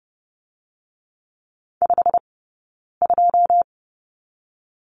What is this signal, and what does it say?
Morse code, heard as a keyed tone, about 30 wpm, 700 Hz. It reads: 5 2
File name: 092 Lesson 16d - Letter c - Two Characters 30wpm